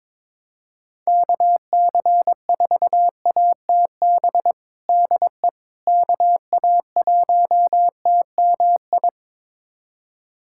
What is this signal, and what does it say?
Morse code, heard as a keyed tone, about 22 wpm, 700 Hz. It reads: KC4ATB DE KA1TMI
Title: USA Callsign DE Callsign - 6 Characters - 22wpm